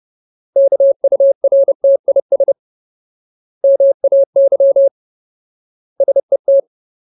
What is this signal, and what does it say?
Morse code, heard as a keyed tone, about 30 wpm, 550 Hz. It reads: KURTIS MAY SET
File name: Ultimate Word List / 30WPM Ultimate